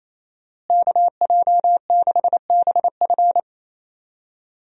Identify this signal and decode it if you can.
Morse code, heard as a keyed tone, about 28 wpm, 700 Hz. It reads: KJ6BF
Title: USA Call Signs - 5 Characters - Repeated 2x - 28wpm